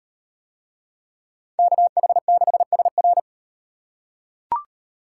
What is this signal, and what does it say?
Morse code, heard as a keyed tone, about 36 wpm, 700 Hz. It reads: KH6SR